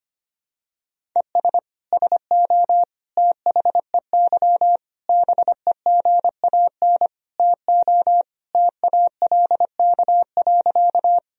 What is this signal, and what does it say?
Morse code, heard as a keyed tone, about 25 wpm, 700 Hz. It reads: ES SO THEY BEGAN TO TALK.
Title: Sentences from Top 400 Words and Abbreviations 25wpm